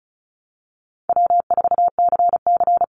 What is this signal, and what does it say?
Morse code, heard as a keyed tone, about 36 wpm, 700 Hz. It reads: W4CC
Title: Call Signs - 35wpm